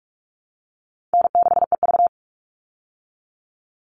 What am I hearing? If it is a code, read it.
Morse code, heard as a keyed tone, about 44 wpm, 700 Hz. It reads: N6EV
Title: Call Signs Repeated 3x - 45wpm